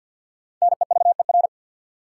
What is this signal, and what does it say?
Morse code, heard as a keyed tone, about 48 wpm, 700 Hz. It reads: NEVER
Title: Sentences from Top 300 Words - New Word Drill - 50wpm